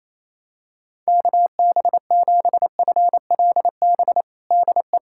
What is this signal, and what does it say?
Morse code, heard as a keyed tone, about 27 wpm, 700 Hz. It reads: KB7FLB DE
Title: Practice QSO - Exchange 4 - Ending - 28wpm